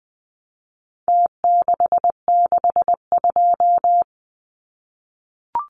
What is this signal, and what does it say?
Morse code, heard as a keyed tone, about 20 wpm, 700 Hz. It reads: T662
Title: Numbers - Contest Style - 4-Digit 20wpm